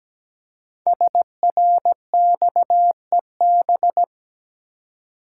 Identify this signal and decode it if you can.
Morse code, heard as a keyed tone, about 17 wpm, 700 Hz. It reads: SRXEB
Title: Code Groups - 5 Letters - 17wpm